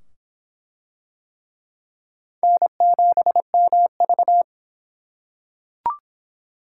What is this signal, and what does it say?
Morse code, heard as a keyed tone, about 23 wpm, 700 Hz. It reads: N7MV